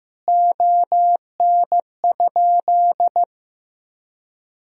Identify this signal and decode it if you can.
Morse code, heard as a keyed tone, about 15 wpm, 700 Hz. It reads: ON?
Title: Sentences from Top 1000 Words - Review Part 3 - 15wpm